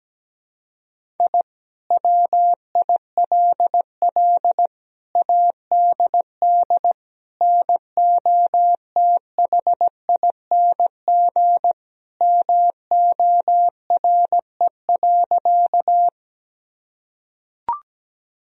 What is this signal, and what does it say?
Morse code, heard as a keyed tone, about 17 wpm, 700 Hz. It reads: I WILL ADD NOTHING MORE.
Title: Sentences from Top 1000 Words - Review Part 1 - 17wpm